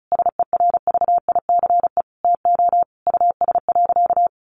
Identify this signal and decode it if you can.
Morse code, heard as a keyed tone, about 34 wpm, 700 Hz. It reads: SERVICE TO US.
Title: Sentences from Top 1200 Words 35wpm